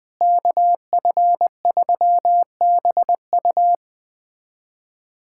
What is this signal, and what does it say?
Morse code, heard as a keyed tone, about 21 wpm, 700 Hz. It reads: KF3BU
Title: USA Callsign DE Callsign - 5 Characters - 20wpm